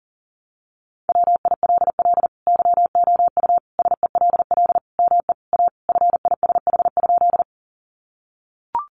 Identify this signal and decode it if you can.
Morse code, heard as a keyed tone, about 40 wpm, 700 Hz. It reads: WILL YOU SELL ME A FISH?
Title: Sentences from Top 1100 Words 40wpm